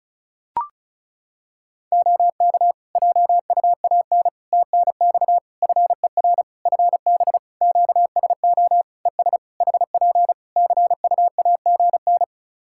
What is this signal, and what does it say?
Morse code, heard as a keyed tone, about 36 wpm, 700 Hz. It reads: OK JUAN TNX FER FB QSO ES HP CUAGN
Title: K1USN SST QSO Practice 35wpm